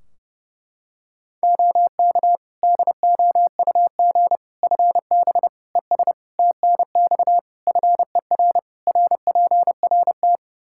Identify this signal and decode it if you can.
Morse code, heard as a keyed tone, about 30 wpm, 700 Hz. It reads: OK DOUG FB ES TNX FER RPRT
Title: K1USN SST QSO Practice 30wpm